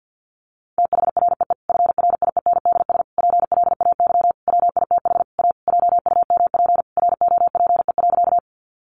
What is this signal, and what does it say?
Morse code, heard as a keyed tone, about 50 wpm, 700 Hz. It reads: THREE FRIENDS PLAY WITH A JUMP ROPE.
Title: Easy Phrases 50wpm